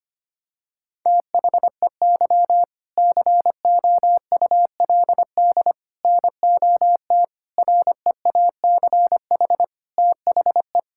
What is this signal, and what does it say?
Morse code, heard as a keyed tone, about 25 wpm, 700 Hz. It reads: THEY COULD NOT REACH THE
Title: Sentences from Top 1000 Words - Review Part 6 - 25wpm